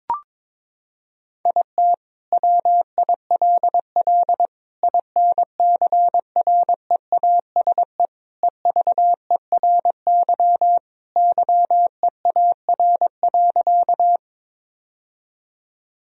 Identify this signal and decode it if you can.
Morse code, heard as a keyed tone, about 22 wpm, 700 Hz. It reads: IT WILL INCREASE EVERY YEAR.
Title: Sentences from Top 1000 Words - Review Part 4 - 22wpm